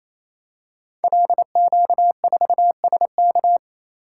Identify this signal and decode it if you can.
Morse code, heard as a keyed tone, about 29 wpm, 700 Hz. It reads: LQ4SK
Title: Code Groups - 5 Letter:Numbers - 28wpm